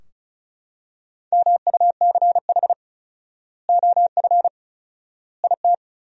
Morse code, heard as a keyed tone, about 36 wpm, 700 Hz. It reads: MUCH OF IT
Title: 4x Word-Spacing - Sentences from Top 400 Words - 35wpm